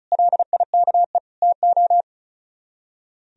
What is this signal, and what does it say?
Morse code, heard as a keyed tone, about 44 wpm, 700 Hz. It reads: LIKE TO